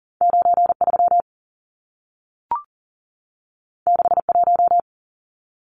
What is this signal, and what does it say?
Morse code, heard as a keyed tone, about 40 wpm, 700 Hz. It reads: 93 61
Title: Numbers - 2-Digit 40wpm